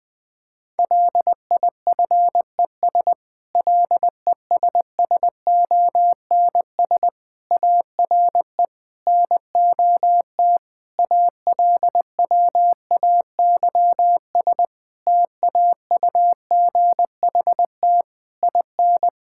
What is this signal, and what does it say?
Morse code, heard as a keyed tone, about 20 wpm, 700 Hz. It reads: LIFES LESSONS ARE NOT ALWAYS TAUGHT IN
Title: Difficult Sentences with Hints - 20wpm